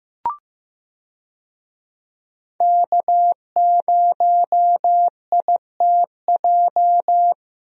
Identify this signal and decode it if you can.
Morse code, heard as a keyed tone, about 15 wpm, 700 Hz. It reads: K0ITJ
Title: USA Call Signs - 5 Characters - Repeated 3x - 15wpm